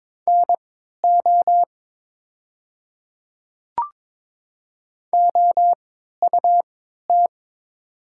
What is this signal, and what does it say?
Morse code, heard as a keyed tone, about 22 wpm, spaced as for 12 wpm, 700 Hz. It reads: NO OUT
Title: 3x Character Spacing - Top 100 Words - 22wpm